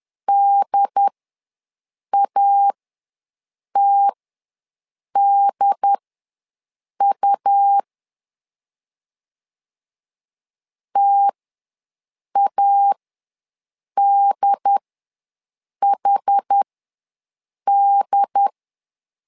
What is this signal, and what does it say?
Morse code, heard as a keyed tone, about 10 wpm, spaced as for 6 wpm, 800 Hz. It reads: DATDU TADHD
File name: LEZIONI / 2 UHDN / 2 UHD ws 10wpm 2058